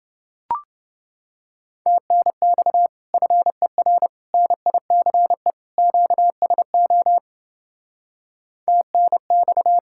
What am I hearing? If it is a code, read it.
Morse code, heard as a keyed tone, about 28 wpm, 700 Hz. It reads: TNX FER NICE QSO TNX